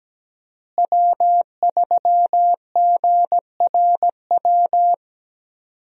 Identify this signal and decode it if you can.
Morse code, heard as a keyed tone, about 17 wpm, 700 Hz. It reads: W3GRW
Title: USA Call Signs - 5 Characters - Repeated 2x - 17wpm